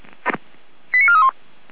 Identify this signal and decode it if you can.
Five-tone selective call (ZVEI), heard as quick runs of five tones, tones of about 70 ms. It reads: 89431